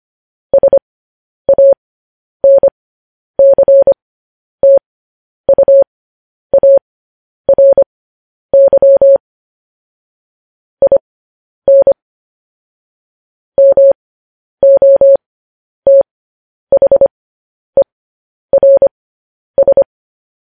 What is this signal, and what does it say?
Morse code, heard as a keyed tone, about 25 wpm, spaced as for 10 wpm, 550 Hz. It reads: SANCTUARY IN MOTHERS
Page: news-headlines-in-morse-code-at-10-wpm